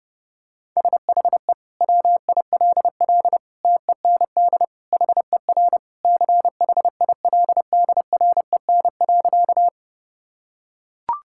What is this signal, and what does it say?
Morse code, heard as a keyed tone, about 32 wpm, 700 Hz. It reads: SHE WILL TEND HER CHILDREN.
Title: Sentences from Top 1000 Words - Review Part 5 - 30wpm